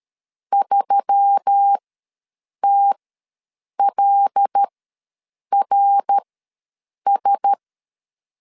Morse code, heard as a keyed tone, about 13 wpm, spaced as for 7 wpm, 800 Hz. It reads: 3TLRS
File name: LEZIONI / 5 LBVC / 5 L ws 12wpm 2958